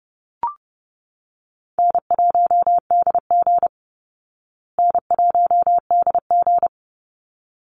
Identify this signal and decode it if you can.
Morse code, heard as a keyed tone, about 30 wpm, 700 Hz. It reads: N1DG N1DG
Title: Call Signs Repeated 3x - 30wpm